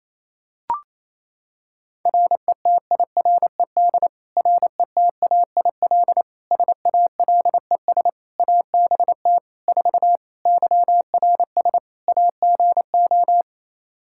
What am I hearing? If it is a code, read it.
Morse code, heard as a keyed tone, about 27 wpm, 700 Hz. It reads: RETIRED RETAIL SALES ABT 4 YRS AGO
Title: Practice QSO - Exchange 3 - More Info - 28wpm